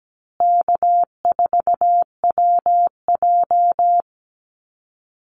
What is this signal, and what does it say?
Morse code, heard as a keyed tone, about 17 wpm, 700 Hz. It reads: K4WJ
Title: Call Signs Repeated 2x - 17wpm